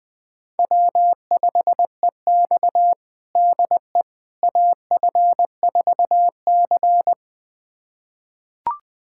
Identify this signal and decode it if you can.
Morse code, heard as a keyed tone, about 20 wpm, 700 Hz. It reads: W5EX DE AF4C
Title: USA Callsign DE Callsign - 4 Characters - 20wpm